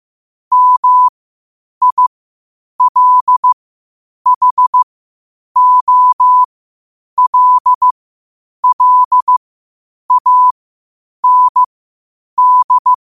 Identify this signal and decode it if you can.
Morse code, heard as a keyed tone, about 15 wpm, spaced as for 8 wpm, 1000 Hz. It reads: MILHOLLAND
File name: qotdfiles / 8wpm